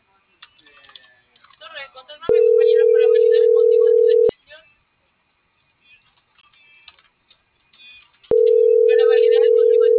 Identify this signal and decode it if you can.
Ringback tone (North America), 440 + 480 Hz.